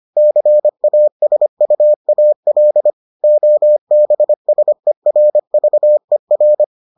Morse code, heard as a keyed tone, about 25 wpm, 600 Hz. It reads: CASUAL OBSERVER